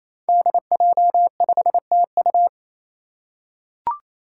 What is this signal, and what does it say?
Morse code, heard as a keyed tone, about 29 wpm, 700 Hz. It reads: DJ5TU
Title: Five Challenging Letter-Number Combinations 28wpm